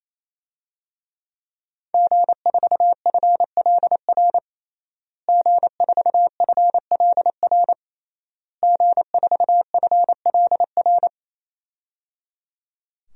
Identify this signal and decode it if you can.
Morse code, heard as a keyed tone, about 28 wpm, 700 Hz. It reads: G4FLR G4FLR G4FLR
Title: DXCC-100 Call Signs - 5 Characters - Repeated 3x - 28wpm